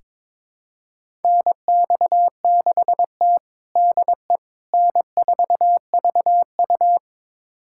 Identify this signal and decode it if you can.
Morse code, heard as a keyed tone, about 22 wpm, 700 Hz. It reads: NX6T DE N4VU
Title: USA Callsign DE Callsign - 4 Characters - 22wpm